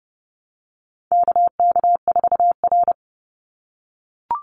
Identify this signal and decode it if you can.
Morse code, heard as a keyed tone, about 30 wpm, 700 Hz. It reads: KK4R
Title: Call Signs - 30wpm